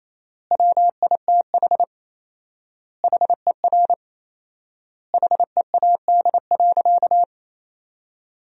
Morse code, heard as a keyed tone, about 27 wpm, 700 Hz. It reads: WITH HER HEAD.
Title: 4x Word-Spacing - Sentences from Top 300 Words - 28wpm